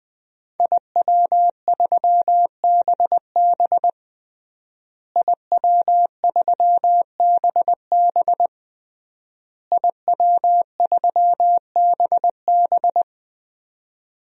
Morse code, heard as a keyed tone, about 20 wpm, 700 Hz. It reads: IW3BB IW3BB IW3BB
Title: DXCC-100 Call Signs - 5 Characters - Repeated 3x - 20wpm